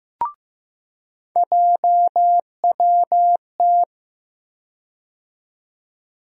Morse code, heard as a keyed tone, about 15 wpm, 700 Hz. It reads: JWT
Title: USA Call Sign Postfixes - 3 Letters 15wpm